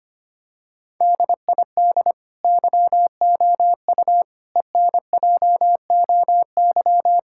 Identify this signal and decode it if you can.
Morse code, heard as a keyed tone, about 25 wpm, 700 Hz. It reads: DID YOU ENJOY
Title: Sentences from Top 1000 Words - Review Part 5 - 25wpm